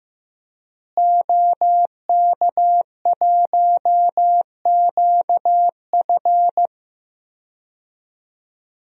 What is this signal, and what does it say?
Morse code, heard as a keyed tone, about 15 wpm, 700 Hz. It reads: OK1QF
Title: DXCC-100 Call Signs - 5 Characters - Repeated 2x - 15wpm